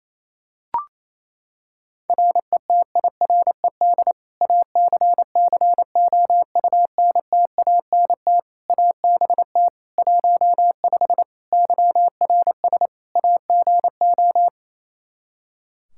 Morse code, heard as a keyed tone, about 27 wpm, 700 Hz. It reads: RETIRED ACCOUNTANT ABT 15 YRS AGO
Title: Practice QSO - Exchange 3 - More Info - 28wpm